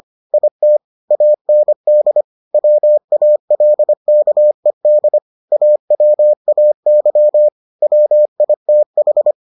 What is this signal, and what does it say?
Morse code, heard as a keyed tone, about 25 wpm, 600 Hz. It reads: IT AND WALKED AWAY WITH